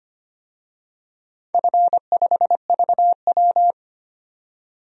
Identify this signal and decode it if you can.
Morse code, heard as a keyed tone, about 25 wpm, 700 Hz. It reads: F5VW
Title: DXCC-100 Call Signs - 4 Characters - 25wpm